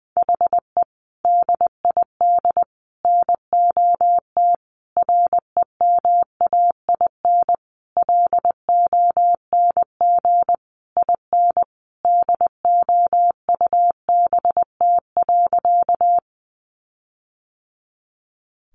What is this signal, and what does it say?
Morse code, heard as a keyed tone, about 20 wpm, 700 Hz. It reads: HE DID NOT REMAIN LONG IN DOUBT.
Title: Sentences from Top 1500 Words 20wpm